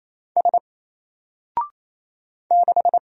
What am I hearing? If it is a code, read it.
Morse code, heard as a keyed tone, about 27 wpm, 700 Hz. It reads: S 6
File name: Single-Letter-Number-28wpm